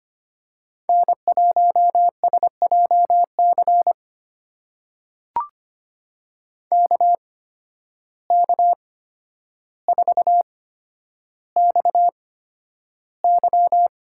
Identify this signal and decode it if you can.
Morse code, heard as a keyed tone, about 25 wpm, spaced as for 11 wpm, 700 Hz. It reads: N1SJC KK4XY
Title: USA Call Signs - 5 Characters - 8x Character Spacing - 25wpm